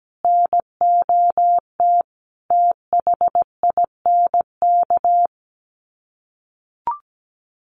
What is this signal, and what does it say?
Morse code, heard as a keyed tone, about 17 wpm, 700 Hz. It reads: NOT THINK
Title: Sets of 5 Words Encore 17wpm